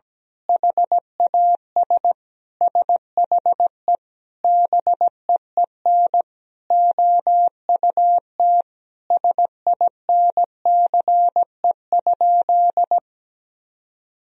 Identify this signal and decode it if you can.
Morse code, heard as a keyed tone, about 17 wpm, 700 Hz. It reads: HAS SHE BEEN OUT SINCE?
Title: Sentences from Top 1000 Words - Review Part 1 - 17wpm